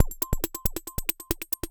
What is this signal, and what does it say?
There is no code.